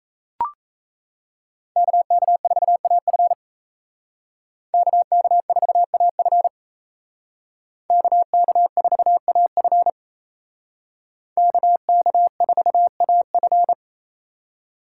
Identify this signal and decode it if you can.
Morse code, heard as a keyed tone, about 35 wpm, 700 Hz. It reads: KK4AF KK4AF KK4AF KK4AF
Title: Speed Racing - USA 2x2 Call Signs 28wpm